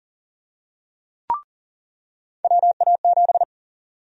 Morse code, heard as a keyed tone, about 40 wpm, 700 Hz. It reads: WA7
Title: USA Call Sign Prefixes - 40wpm